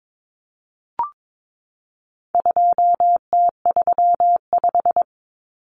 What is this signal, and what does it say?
Morse code, heard as a keyed tone, about 22 wpm, 700 Hz. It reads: 2T35
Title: Common Cut Numbers - 4-Digit 22wpm